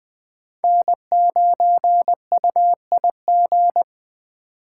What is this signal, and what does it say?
Morse code, heard as a keyed tone, about 21 wpm, 700 Hz. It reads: N9UIG
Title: Speed Racing - USA 1x3 Call Signs 17wpm